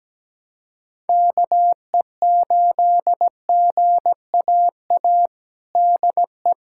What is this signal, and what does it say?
Morse code, heard as a keyed tone, about 17 wpm, 700 Hz. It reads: KE8GAA DE
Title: USA Callsign DE Callsign 17wpm